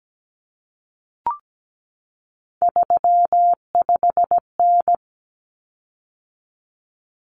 Morse code, heard as a keyed tone, about 17 wpm, 700 Hz. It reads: 35N
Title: Common Cut Numbers - 3-Digit 17wpm